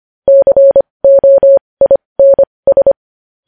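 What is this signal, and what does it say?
Morse code, heard as a keyed tone, about 25 wpm, spaced as for 20 wpm, 550 Hz. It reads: COINS
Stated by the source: Morse news headlines sent at 20.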